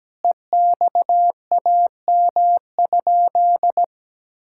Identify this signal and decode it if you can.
Morse code, heard as a keyed tone, about 17 wpm, 700 Hz. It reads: EXAM?
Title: Sentences from Top 1000 Words - Review Part 5 - 17wpm